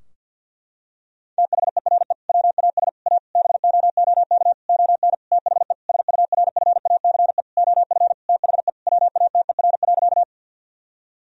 Morse code, heard as a keyed tone, about 51 wpm, 700 Hz. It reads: THERE WAS A BOOK ON THE SURFACE OF THE WATER.
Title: Sentences from Top 1000 Words - Review Part 3 - 50wpm